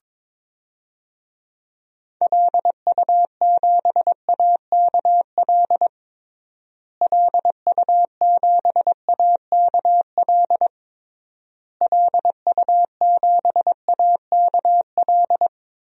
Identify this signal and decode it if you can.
Morse code, heard as a keyed tone, about 22 wpm, 700 Hz. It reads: LU7AKL LU7AKL LU7AKL
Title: DXCC-100 Call Signs - 6 Characters - Repeated 3x - 22wpm